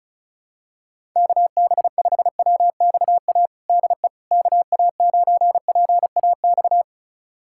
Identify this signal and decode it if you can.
Morse code, heard as a keyed tone, about 36 wpm, 700 Hz. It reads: KB5WXA DE KA9PAX
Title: USA Callsign DE Callsign - 6 Characters - 35wpm